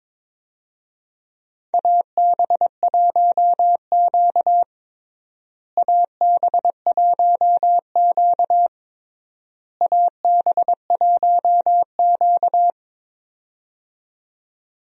Morse code, heard as a keyed tone, about 22 wpm, 700 Hz. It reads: AB1Q AB1Q AB1Q
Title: USA Call Signs - 4 Characters - Repeated 3x - 22wpm